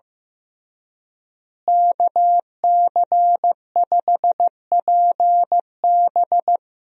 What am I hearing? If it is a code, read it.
Morse code, heard as a keyed tone, about 15 wpm, 700 Hz. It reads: KC5PB
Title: USA Call Signs - 5 Characters - Repeated 3x - 15wpm